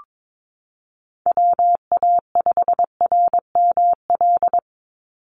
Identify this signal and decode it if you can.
Morse code, heard as a keyed tone, about 22 wpm, 700 Hz. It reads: WA5RML
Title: Call Signs - 22wpm